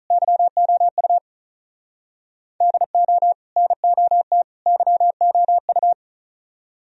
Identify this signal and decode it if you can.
Morse code, heard as a keyed tone, about 44 wpm, 700 Hz. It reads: YOU DO NOT YOU